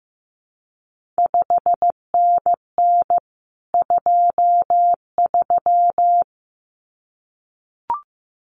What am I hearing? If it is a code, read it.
Morse code, heard as a keyed tone, about 15 wpm, 700 Hz. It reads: 5NN 23
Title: CQ WW DX Contest Exchange Encore - 15wpm